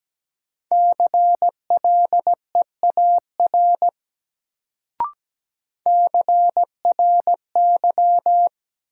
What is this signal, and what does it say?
Morse code, heard as a keyed tone, about 17 wpm, 700 Hz. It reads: CLEAR CRY
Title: Sets of 5 Words Encore - New Word Drill - 17wpm